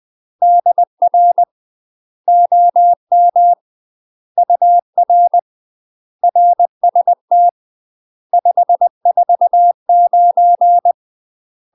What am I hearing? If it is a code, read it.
Morse code, heard as a keyed tone, about 20 wpm, 700 Hz. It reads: DR OM UR RST 549
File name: UK MASTER QSOs TRAINING/20WPM